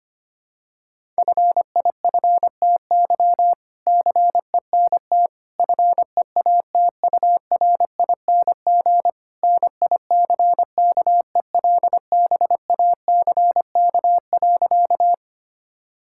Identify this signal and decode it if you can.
Morse code, heard as a keyed tone, about 25 wpm, 700 Hz. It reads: FIFTY CENT FEATURING NICKELBACK.
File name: practice / 25wpm / All-Age Appropriate Jokes - 25wpm